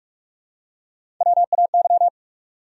Morse code, heard as a keyed tone, about 45 wpm, 700 Hz. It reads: WAY
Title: Top 100 Words 45wpm